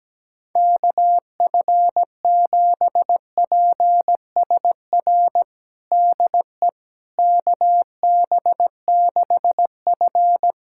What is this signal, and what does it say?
Morse code, heard as a keyed tone, about 17 wpm, 700 Hz. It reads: KF7PSR DE KB6F
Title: USA Callsign DE Callsign 17wpm